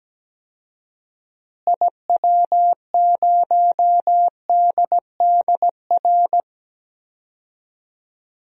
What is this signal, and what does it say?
Morse code, heard as a keyed tone, about 17 wpm, 700 Hz. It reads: IW0DDR
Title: DXCC-100 Call Signs - 6 Characters - Repeated 3x - 17wpm